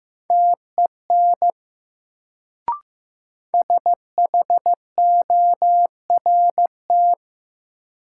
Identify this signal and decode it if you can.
Morse code, heard as a keyed tone, about 15 wpm, 700 Hz. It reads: TEN SHORT
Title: Sentences from Top 400 Words - New Word Drill - 15wpm